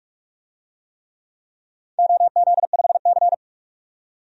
Morse code, heard as a keyed tone, about 46 wpm, 700 Hz. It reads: OZ5C
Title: DXCC-100 Call Signs - 4 Characters - 45wpm